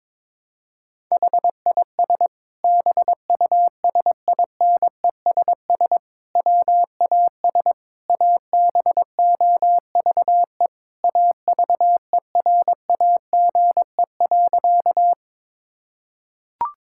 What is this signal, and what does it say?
Morse code, heard as a keyed tone, about 22 wpm, 700 Hz. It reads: HIS BUSINESS WAS ABOVE AVERAGE.
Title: Sentences from Top 1000 Words - Review Part 6 - 22wpm